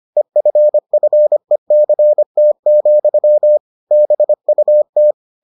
Morse code, heard as a keyed tone, about 25 wpm, 600 Hz. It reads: EFFECT, BUT